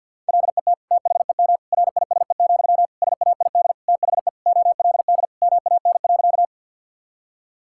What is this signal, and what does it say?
Morse code, heard as a keyed tone, about 50 wpm, 700 Hz. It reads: LET THEM RISE, SAID THE OLD MAN.
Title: Sentences from Top 1000 Words - Review Part 6 - 50wpm